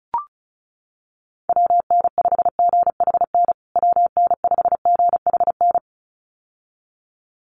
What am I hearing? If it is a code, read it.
Morse code, heard as a keyed tone, about 34 wpm, 700 Hz. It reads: WN5GHN WN5GHN
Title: USA Call Signs Repeated 2x 35wpm